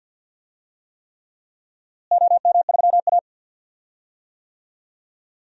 Morse code, heard as a keyed tone, about 50 wpm, 700 Hz. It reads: OM3A
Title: DXCC-100 Call Signs - 4 Characters - Repeated 3x - 50wpm